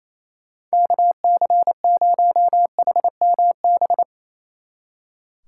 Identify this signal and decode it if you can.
Morse code, heard as a keyed tone, about 34 wpm, 700 Hz. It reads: KC0HMB
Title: Speed Racing - USA 2x3 Call Signs 28wpm